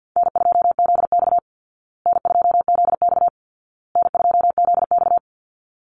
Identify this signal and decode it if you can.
Morse code, heard as a keyed tone, about 50 wpm, 700 Hz. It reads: N2ZX N2ZX N2ZX
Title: Call Signs Repeated 2x - 50wpm